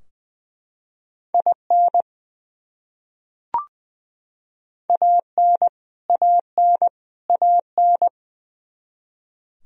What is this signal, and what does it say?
Morse code, heard as a keyed tone, about 20 wpm, 700 Hz. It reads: IN AN AN AN
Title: Top 100 Words Repeated 20wpm